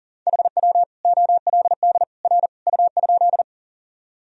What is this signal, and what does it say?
Morse code, heard as a keyed tone, about 40 wpm, 700 Hz. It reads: HW OLD R U?
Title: Sentences from Top 200 Words and Abbreviations 40wpm